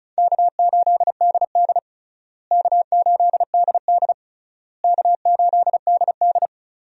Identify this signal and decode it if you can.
Morse code, heard as a keyed tone, about 36 wpm, 700 Hz. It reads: K8DD K8DD K8DD
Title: USA Call Signs - 4 Characters - Repeated 3x - 35wpm